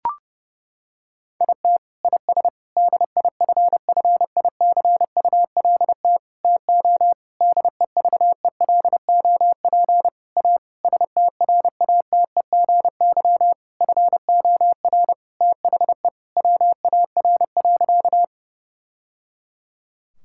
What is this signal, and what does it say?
Morse code, heard as a keyed tone, about 32 wpm, 700 Hz. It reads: IT IS DIFFICULT TO DEVELOP A STRATEGY FOR THE WAR.
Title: Sentences from Top 1000 Words - Review Part 5 - 30wpm